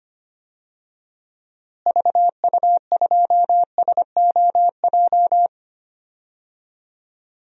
Morse code, heard as a keyed tone, about 25 wpm, 700 Hz. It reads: VU2SOJ
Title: DXCC-100 Call Signs - 6 Characters - Repeated 2x - 25wpm